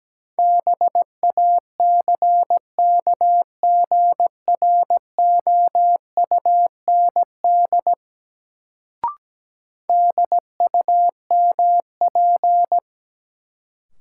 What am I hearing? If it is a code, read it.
Morse code, heard as a keyed tone, about 17 wpm, 700 Hz. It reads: BACKGROUND DUMP
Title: Sentences from Top 1300 Words - New Word Drill - 17wpm